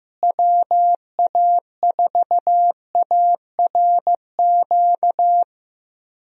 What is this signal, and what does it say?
Morse code, heard as a keyed tone, about 15 wpm, 700 Hz. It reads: WA4ARQ
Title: USA Call Signs - 6 Characters - Repeated 3x - 15wpm